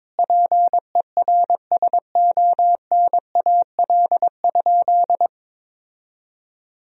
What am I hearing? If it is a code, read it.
Morse code, heard as a keyed tone, about 22 wpm, 700 Hz. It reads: PERSONAL?
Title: Sentences from Top 1000 Words - Review Part 4 - 22wpm